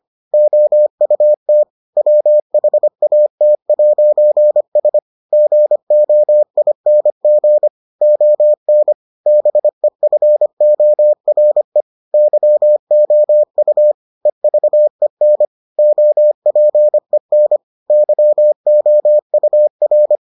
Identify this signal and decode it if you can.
Morse code, heard as a keyed tone, about 25 wpm, 600 Hz. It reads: OUT WHAT'S GOING ON BEFORE YOU EVEN OPEN YOUR